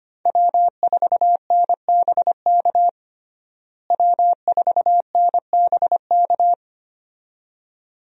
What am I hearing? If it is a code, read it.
Morse code, heard as a keyed tone, about 25 wpm, 700 Hz. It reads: W4NBK W4NBK
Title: USA Call Signs - 5 Characters - Repeated 2x - 25wpm